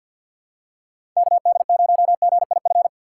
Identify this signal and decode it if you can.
Morse code, heard as a keyed tone, about 50 wpm, 690 Hz. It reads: KD0GIF
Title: USA Call Signs - 6x Character Spacing - 50wpm